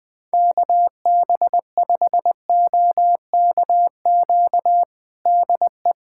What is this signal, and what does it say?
Morse code, heard as a keyed tone, about 20 wpm, 700 Hz. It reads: KB5OKQ DE
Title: USA Callsign DE Callsign 20wpm